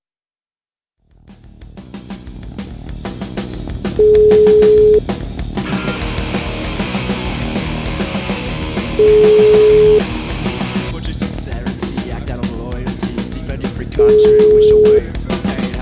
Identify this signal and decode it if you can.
Ringback tone (Europe, 425 Hz), with speech.